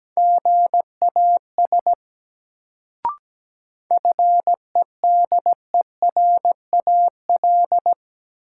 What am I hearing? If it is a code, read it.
Morse code, heard as a keyed tone, about 17 wpm, 700 Hz. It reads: GAS FEDERAL
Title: Sentences from Top 1100 Words - New Word Drill Alt - 17wpm